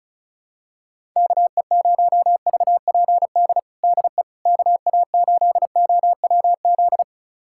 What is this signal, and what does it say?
Morse code, heard as a keyed tone, about 34 wpm, 700 Hz. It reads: KE0VPD DE KA8OWZ
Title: USA Callsign DE Callsign - 6 Characters - 35wpm